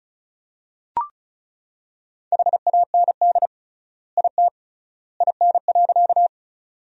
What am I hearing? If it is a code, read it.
Morse code, heard as a keyed tone, about 34 wpm, 700 Hz. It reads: HAND IT IN.
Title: 3x Word-Spacing - Sentences from Top 200 Words - 35wpm